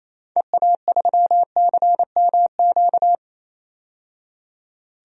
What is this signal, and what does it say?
Morse code, heard as a keyed tone, about 28 wpm, 700 Hz. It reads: EA3CMQ
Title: DXCC-100 Call Signs - 6 Characters - 28wpm